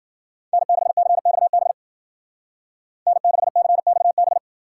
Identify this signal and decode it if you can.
Morse code, heard as a keyed tone, about 48 wpm, 700 Hz. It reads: N6CXB N6CXB
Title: Speed Racing - USA 1x3 Call Signs 40wpm